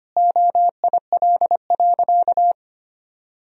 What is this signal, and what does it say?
Morse code, heard as a keyed tone, about 25 wpm, 700 Hz. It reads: OIL.
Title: Sentences from Top 1000 Words - Review Part 4 - 25wpm